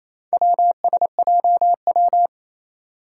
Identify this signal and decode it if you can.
Morse code, heard as a keyed tone, about 27 wpm, 700 Hz. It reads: WSJW
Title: Four Challenging Letter-Number Combinations 28wpm